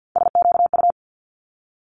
Morse code, heard as a keyed tone, about 50 wpm, 700 Hz. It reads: SQU
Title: Multi-letter Phonemes Quick Fire 50wpm